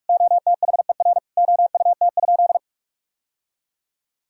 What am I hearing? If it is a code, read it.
Morse code, heard as a keyed tone, about 46 wpm, 700 Hz. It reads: OTHER OUT?
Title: Sentences from Top 1000 Words - Review Part 6 - 45wpm